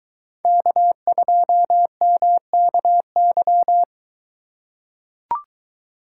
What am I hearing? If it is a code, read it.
Morse code, heard as a keyed tone, about 23 wpm, 700 Hz. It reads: K2MKY